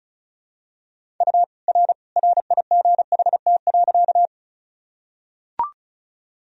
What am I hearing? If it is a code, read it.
Morse code, heard as a keyed tone, about 36 wpm, 700 Hz. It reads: U R RIGHT.
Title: Sentences from Top 200 Words and Abbreviations 35wpm